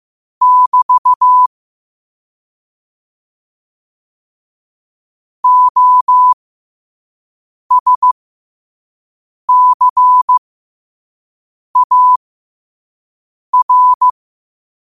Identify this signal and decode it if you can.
Morse code, heard as a keyed tone, about 15 wpm, spaced as for 5 wpm, 1000 Hz. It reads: <BT> OSCAR
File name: 5wpm